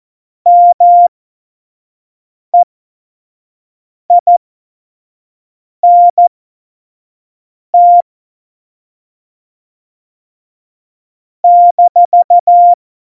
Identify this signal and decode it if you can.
Morse code, heard as a keyed tone, about 14 wpm, spaced as for 5 wpm, 700 Hz. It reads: MEINT -
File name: Morsekurs / Witze / Witz 24. November 2020, 25 F70